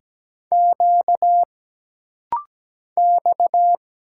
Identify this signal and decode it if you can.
Morse code, heard as a keyed tone, about 17 wpm, 700 Hz. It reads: Q X
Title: Single Letters - Rapid-Fire 17wpm